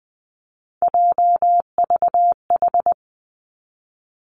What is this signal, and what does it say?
Morse code, heard as a keyed tone, about 20 wpm, 700 Hz. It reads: JVH
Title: Three Challenging Letter-Number Combinations 20wpm